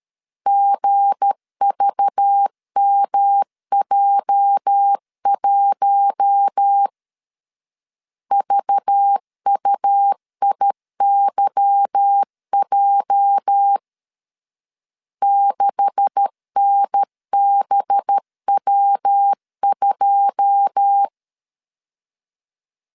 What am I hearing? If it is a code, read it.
Morse code, heard as a keyed tone, about 13 wpm, 800 Hz. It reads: GVMJ1 VUIYJ 6NBW2
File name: CW / DET / 1 9 1500C S C 12wpm 3811